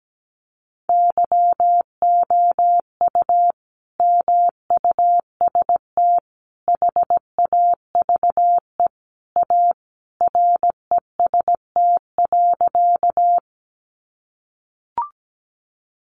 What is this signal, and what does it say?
Morse code, heard as a keyed tone, about 17 wpm, 700 Hz. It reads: YOU MUST HAVE A REST.
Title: Sentences from Top 500 Words 17wpm